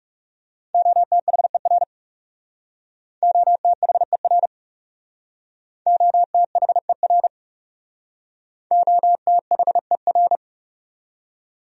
Morse code, heard as a keyed tone, about 39 wpm, 700 Hz. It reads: OTHER OTHER OTHER OTHER